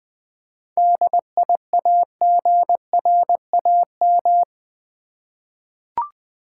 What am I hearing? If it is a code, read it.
Morse code, heard as a keyed tone, about 20 wpm, 700 Hz. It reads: DIAGRAM
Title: 7 Letter Words - First 3 Letters Given - 20wpm